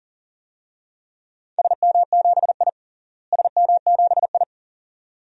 Morse code, heard as a keyed tone, about 40 wpm, 700 Hz. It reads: SM7I SM7I
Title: DXCC-100 Call Signs - 4 Characters - Repeated 2x - 40wpm